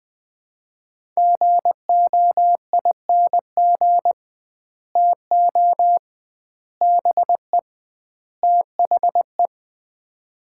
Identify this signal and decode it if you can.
Morse code, heard as a keyed tone, about 20 wpm, 700 Hz. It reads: GOING TO BE THE
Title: 2x Word-Spacing - Sets of 4 Words - 20wpm